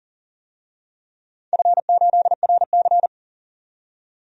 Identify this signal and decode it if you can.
Morse code, heard as a keyed tone, about 37 wpm, 700 Hz. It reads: F8RC
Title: DXCC-100 Call Signs - 4 Characters - 40wpm